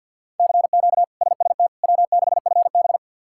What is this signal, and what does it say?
Morse code, heard as a keyed tone, about 46 wpm, 700 Hz. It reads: CQ SST W6FB
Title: K1USN SST QSO Practice - 50wpm